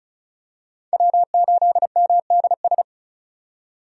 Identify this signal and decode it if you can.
Morse code, heard as a keyed tone, about 36 wpm, 700 Hz. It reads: W8MDS
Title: USA Call Signs - 5 Characters - 35wpm